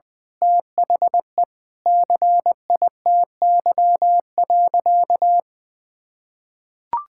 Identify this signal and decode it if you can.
Morse code, heard as a keyed tone, about 20 wpm, 700 Hz. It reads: THE CITY.
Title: Sentences with 3-Word Phrases - 20wpm